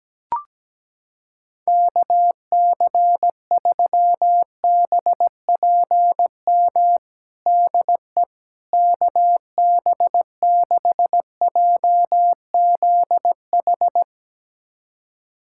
Morse code, heard as a keyed tone, about 17 wpm, 700 Hz. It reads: KC3BPM DE KB6JZH
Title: USA Callsign DE Callsign - 6 Characters - 17wpm